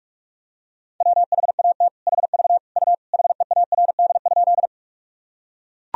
Morse code, heard as a keyed tone, about 41 wpm, 700 Hz. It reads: WHAT HV U HEARD?
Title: Sentences from Top 400 Words and Abbreviations 45wpm